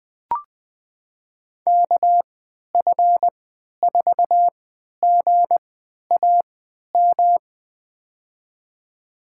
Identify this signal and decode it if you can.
Morse code, heard as a keyed tone, about 20 wpm, spaced as for 11 wpm, 700 Hz. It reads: KF4GAM
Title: USA Call Signs - 6 Characters - 3x Character Spacing - 20wpm